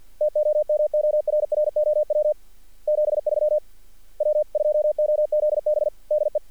Morse code, heard as a keyed tone, about 46 wpm, 600 Hz. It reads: TOMORROW 73 W2OZB DE
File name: archived / CRSnet